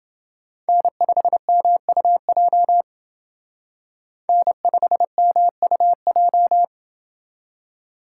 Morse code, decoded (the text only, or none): N5MUJ N5MUJ